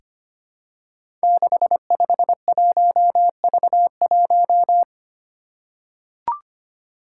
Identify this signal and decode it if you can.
Morse code, heard as a keyed tone, about 25 wpm, 700 Hz. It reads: 651V1
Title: Five Challenging Letter-Number Combinations 25wpm